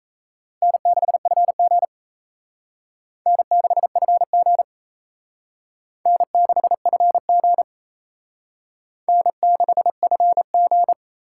Morse code, heard as a keyed tone, about 35 wpm, 700 Hz. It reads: N6FG N6FG N6FG N6FG
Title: Speed Racing - USA 1x2 Call Signs 28wpm